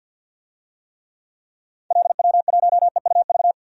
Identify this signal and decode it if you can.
Morse code, heard as a keyed tone, about 51 wpm, 700 Hz. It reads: LW1EUV
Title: DXCC-100 Call Signs - 6 Characters - 50wpm